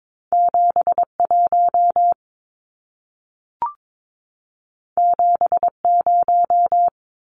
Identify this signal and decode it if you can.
Morse code, heard as a keyed tone, about 22 wpm, 700 Hz. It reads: 71 70
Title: Numbers - 2-Digit 22wpm